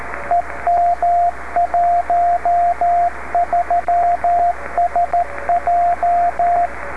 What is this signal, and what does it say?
Morse code, heard as a keyed tone, about 13 wpm, 700 Hz, with radio noise. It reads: EM13SJ
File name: beacons / w5hn-b